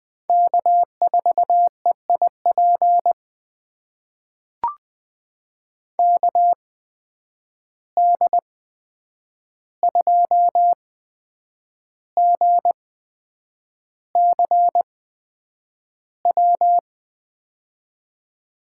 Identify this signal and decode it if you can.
Morse code, heard as a keyed tone, about 20 wpm, spaced as for 8 wpm, 700 Hz. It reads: K4EIP KD2GCW
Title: USA Call Signs - 8x Character Spacing - 20wpm